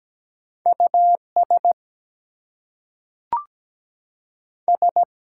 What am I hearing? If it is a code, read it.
Morse code, heard as a keyed tone, about 17 wpm, 700 Hz. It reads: US S
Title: Two Letter Words - 17wpm